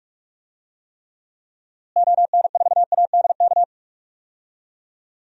Morse code, heard as a keyed tone, about 46 wpm, 700 Hz. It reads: ON4ADK
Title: DXCC-100 Call Signs - 6 Characters - Repeated 3x - 45wpm